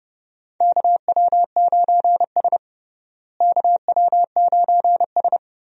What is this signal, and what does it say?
Morse code, heard as a keyed tone, about 30 wpm, 700 Hz. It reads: KW9S KW9S
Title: USA Call Signs - 4 Characters - Repeated 2x - 30wpm